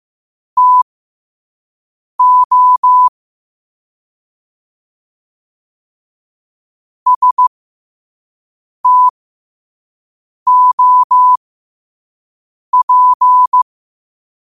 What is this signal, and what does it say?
Morse code, heard as a keyed tone, about 15 wpm, spaced as for 5 wpm, 1000 Hz. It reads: TO STOP
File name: qotdfiles / 5wpm